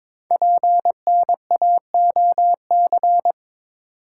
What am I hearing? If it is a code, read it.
Morse code, heard as a keyed tone, about 22 wpm, 700 Hz. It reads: PNAOC
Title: Code Groups - 5 Letters - 22wpm